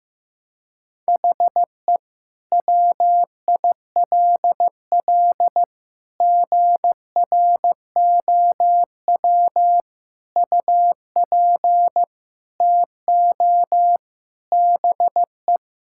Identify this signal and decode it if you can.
Morse code, heard as a keyed tone, about 15 wpm, 700 Hz. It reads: HE WILL GROW UP TO BE
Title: Sentences from Top 1000 Words - Review Part 6 - 15wpm